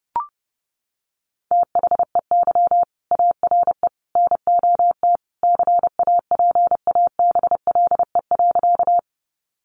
Morse code, heard as a keyed tone, about 30 wpm, 700 Hz. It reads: THEY ARE NOT CAPABLE.
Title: Sentences from Top 1200 Words 30wpm